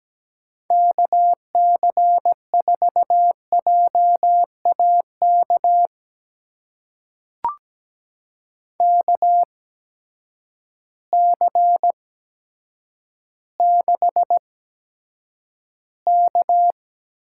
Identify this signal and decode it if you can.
Morse code, heard as a keyed tone, about 17 wpm, spaced as for 8 wpm, 700 Hz. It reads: KC4JAK KC6K
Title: USA Call Signs - 6 Characters - 8x Character Spacing - 17wpm